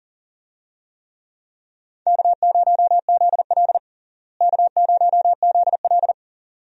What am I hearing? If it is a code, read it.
Morse code, heard as a keyed tone, about 40 wpm, 700 Hz. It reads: K0ZL K0ZL
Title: USA Call Signs - 4 Characters - Repeated 2x - 40wpm